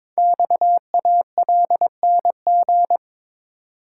Morse code, heard as a keyed tone, about 22 wpm, 700 Hz. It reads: XALNG
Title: Code Groups - 5 Letters - 22wpm